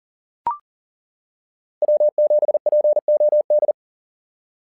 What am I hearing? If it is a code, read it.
Morse code, heard as a keyed tone, about 40 wpm, 600 Hz. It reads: W7POD
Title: SOTA QSO Practice 40wpm